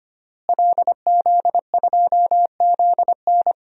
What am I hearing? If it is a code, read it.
Morse code, heard as a keyed tone, about 25 wpm, 700 Hz. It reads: LZ2ZN
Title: DXCC-100 Call Signs - 5 Characters - Repeated 3x - 25wpm